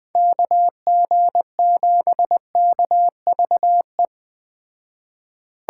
Morse code, heard as a keyed tone, about 19 wpm, 700 Hz. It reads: KG7KVE
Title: Speed Racing - USA 2x3 Call Signs 17wpm